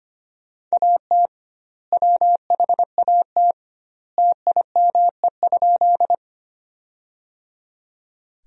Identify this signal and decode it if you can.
Morse code, heard as a keyed tone, about 25 wpm, 700 Hz. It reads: AT WHAT TIME?
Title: 2x Word-Spacing - Sentences from Top 100 Words - 25wpm